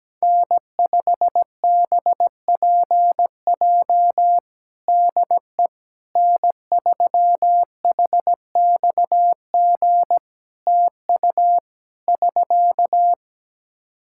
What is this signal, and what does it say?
Morse code, heard as a keyed tone, about 17 wpm, 700 Hz. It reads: N5BPJ DE N3HXG TU <SK>
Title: K1USN SST QSO Practice 17wpm